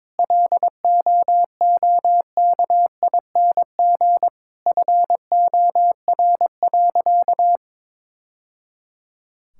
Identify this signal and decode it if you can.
Morse code, heard as a keyed tone, about 22 wpm, 700 Hz. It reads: LOOKING FOR.
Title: Sentences from Top 1000 Words - Review Part 6 - 22wpm